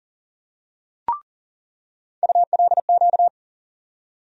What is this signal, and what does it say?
Morse code, heard as a keyed tone, about 40 wpm, 700 Hz. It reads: ULQ
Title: USA Call Sign Postfixes - 3 Letters 40wpm